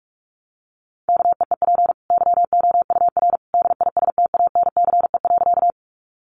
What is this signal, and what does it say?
Morse code, heard as a keyed tone, about 44 wpm, 700 Hz. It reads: KEEP YOUR DISTANCE.
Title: Sentences from Top 1200 Words 45wpm